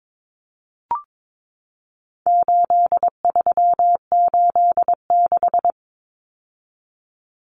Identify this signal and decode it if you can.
Morse code, heard as a keyed tone, about 22 wpm, 700 Hz. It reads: 8386
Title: Numbers - 4-Digit 22wpm